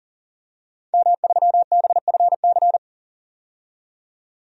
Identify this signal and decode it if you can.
Morse code, heard as a keyed tone, about 40 wpm, 700 Hz. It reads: M3BFC
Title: DXCC-100 Call Signs - 5 Characters - 40wpm